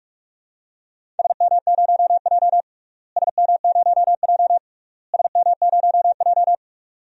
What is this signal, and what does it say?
Morse code, heard as a keyed tone, about 46 wpm, 700 Hz. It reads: SM0J SM0J SM0J
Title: DXCC-100 Call Signs - 4 Characters - Repeated 3x - 45wpm